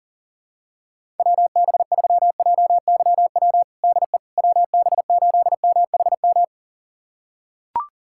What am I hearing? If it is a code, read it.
Morse code, heard as a keyed tone, about 37 wpm, 700 Hz. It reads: WB3JYW DE WB8MHM
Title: USA Callsign DE Callsign - 6 Characters - 40wpm